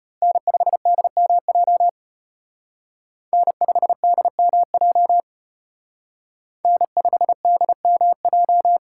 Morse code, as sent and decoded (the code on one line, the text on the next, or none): -. ..... -.. -- .--- / -. ..... -.. -- .--- / -. ..... -.. -- .---
N5DMJ N5DMJ N5DMJ